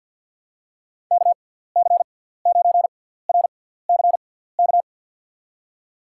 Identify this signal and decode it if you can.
Morse code, heard as a keyed tone, about 52 wpm, spaced as for 17 wpm, 700 Hz. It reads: KC9RCK
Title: USA Call Signs - 6 Characters - 6x Character Spacing - 50wpm